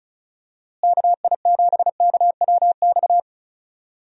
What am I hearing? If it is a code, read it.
Morse code, heard as a keyed tone, about 34 wpm, 700 Hz. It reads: KI7KWX
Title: USA Call Signs - 6 Characters - Repeated 3x - 35wpm